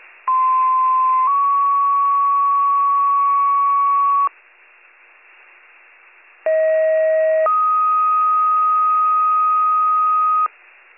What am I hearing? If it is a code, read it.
Two-tone page, heard as a one-second tone, then a three-second tone, about 1040 then 1120 Hz, about 640 then 1250 Hz. It is heard from 0.3 to 10.5 s.